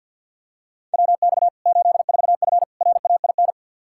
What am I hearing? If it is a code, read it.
Morse code, heard as a keyed tone, about 49 wpm, 700 Hz. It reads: WX 84F RAIN
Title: K1USN SST QSO Practice 50wpm